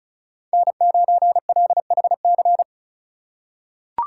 Morse code, heard as a keyed tone, about 36 wpm, 700 Hz. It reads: N9LHC
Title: USA Call Signs - 5 Characters - 2x Character Spacing - 35wpm